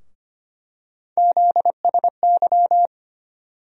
Morse code, heard as a keyed tone, about 25 wpm, 700 Hz. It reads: ZSY
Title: USA Call Sign Postfixes - 3 Letters 25wpm